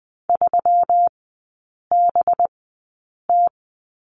Morse code, heard as a keyed tone, about 20 wpm, spaced as for 8 wpm, 700 Hz. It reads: 3BT
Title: 184 Lesson 30e - Letter b - Three Characters 20wpm